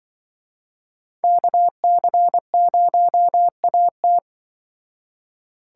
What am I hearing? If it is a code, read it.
Morse code, heard as a keyed tone, about 25 wpm, 700 Hz. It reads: KC0AT
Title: Speed Racing - USA 2x2 Call Signs 20wpm